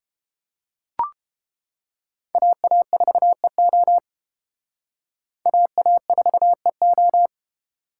Morse code, heard as a keyed tone, about 28 wpm, 700 Hz. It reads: AA4EO AA4EO